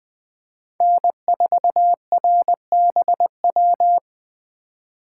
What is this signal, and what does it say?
Morse code, heard as a keyed tone, about 20 wpm, 700 Hz. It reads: N4RBW